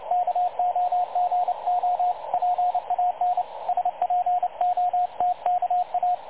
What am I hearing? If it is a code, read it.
Morse code, heard as a keyed tone, about 29 wpm, 720 Hz, with radio noise. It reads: KOGO PAN SPOTKA
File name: dlazaawansowanych / nagraniazpasma / websdr recording 3565.0kHz